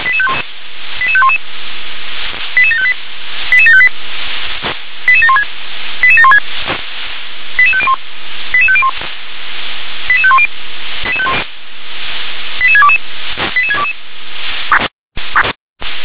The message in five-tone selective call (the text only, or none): 88511 88758 88758 88716 88716 88501 88501 88510 88622 88622